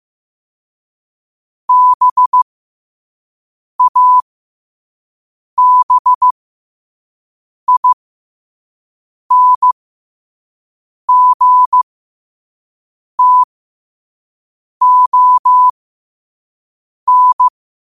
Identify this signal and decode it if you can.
Morse code, heard as a keyed tone, about 15 wpm, spaced as for 5 wpm, 1000 Hz. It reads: BABINGTON
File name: qotdfiles / 5wpm